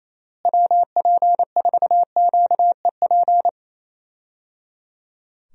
Morse code, heard as a keyed tone, about 28 wpm, 700 Hz. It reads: WP4QEP
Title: USA Call Signs - 6 Characters - Repeated 3x - 28wpm